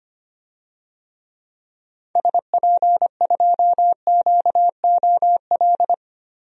Morse code, heard as a keyed tone, about 25 wpm, 700 Hz. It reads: SP2QOL
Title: DXCC-100 Call Signs - 6 Characters - 25wpm